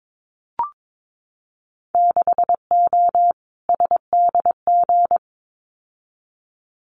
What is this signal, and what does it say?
Morse code, heard as a keyed tone, about 22 wpm, 700 Hz. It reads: 6O SDG
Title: Winter Field Day Exchanges Encore 22wpm